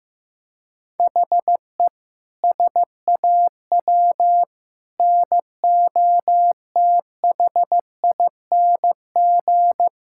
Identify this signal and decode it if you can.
Morse code, heard as a keyed tone, about 15 wpm, 700 Hz. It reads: HE SAW NOTHING
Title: Sentences from Top 1600 Words 15wpm